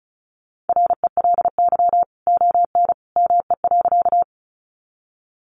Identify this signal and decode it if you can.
Morse code, heard as a keyed tone, about 34 wpm, 700 Hz. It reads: RELY ON ME.
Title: Sentences from Top 1200 Words 35wpm